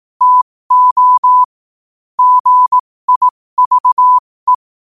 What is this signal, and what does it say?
Morse code, heard as a keyed tone, about 18 wpm, spaced as for 15 wpm, 1000 Hz. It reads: TO GIVE
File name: qotdfiles / 15wpm